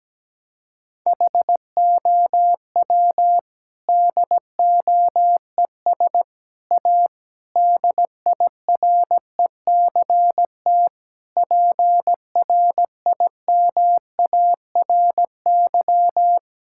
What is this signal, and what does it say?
Morse code, heard as a keyed tone, about 17 wpm, 700 Hz. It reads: HOW DOES A DIRECT PRIMARY
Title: Sentences from Top 1000 Words - Review Part 5 - 17wpm